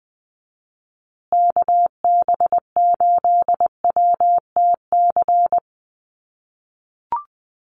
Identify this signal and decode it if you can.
Morse code, heard as a keyed tone, about 20 wpm, 700 Hz. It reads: KB8WTC
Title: USA Call Signs Repeated 2x 20wpm